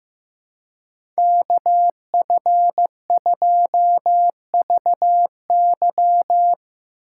Morse code, heard as a keyed tone, about 15 wpm, 700 Hz. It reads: KF2VY
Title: USA Call Signs - 5 Characters - 2x Character Spacing - 15wpm